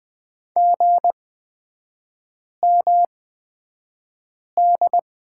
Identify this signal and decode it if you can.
Morse code, heard as a keyed tone, about 20 wpm, spaced as for 5 wpm, 700 Hz. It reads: GMD
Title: 243 Lesson 39e - Letter z - Three Characters 20wpm